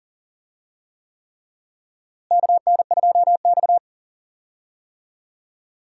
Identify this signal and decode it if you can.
Morse code, heard as a keyed tone, about 37 wpm, 700 Hz. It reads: KN2X
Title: USA Call Signs - 4 Characters - Repeated 3x - 40wpm